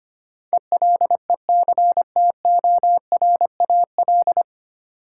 Morse code, heard as a keyed tone, about 25 wpm, 700 Hz. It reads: ELECTORAL
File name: Nine Letter Words - 25wpm